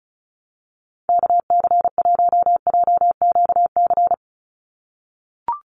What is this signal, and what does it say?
Morse code, heard as a keyed tone, about 34 wpm, 700 Hz. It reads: KC1JQC
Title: USA Call Signs 35wpm